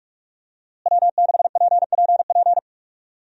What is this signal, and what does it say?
Morse code, heard as a keyed tone, about 46 wpm, 700 Hz. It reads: W6PPP
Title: USA Call Signs - 5 Characters - Repeated 2x - 45wpm